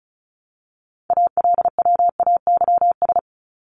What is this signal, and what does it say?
Morse code, heard as a keyed tone, about 34 wpm, 700 Hz. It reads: ALWAYS
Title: Top 300 Words Repeated 35wpm